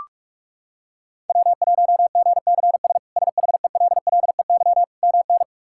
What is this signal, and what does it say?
Morse code, heard as a keyed tone, about 46 wpm, 700 Hz. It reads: W1GCS SHELLEY MN
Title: K1USN SST QSO Practice - 45wpm